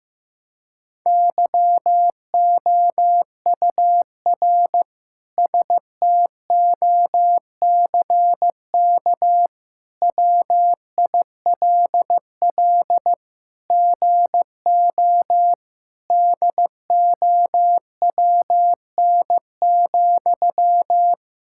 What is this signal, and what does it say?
Morse code, heard as a keyed tone, about 16 wpm, 700 Hz. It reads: YOUR STOCK WILL GO DOWN,
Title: Sentences from Top 1000 Words - Review Part 5 - 15wpm